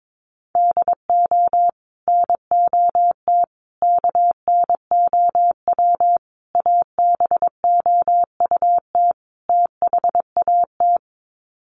Morse code, heard as a keyed tone, about 22 wpm, 700 Hz. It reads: DO NOT KNOW ABOUT THAT
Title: Sets of 5 Words Encore2 22wpm